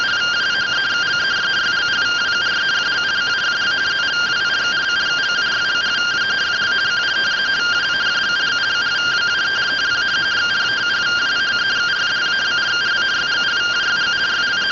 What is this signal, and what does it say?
Two-tone teleprinter signal, about 170 Hz shift.